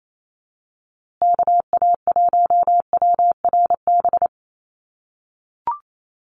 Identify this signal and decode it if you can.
Morse code, heard as a keyed tone, about 28 wpm, 700 Hz. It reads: KA1WRB
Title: USA Call Signs Repeated 3x 28wpm